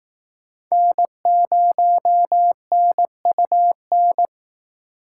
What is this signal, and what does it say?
Morse code, heard as a keyed tone, about 19 wpm, 700 Hz. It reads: N0NUN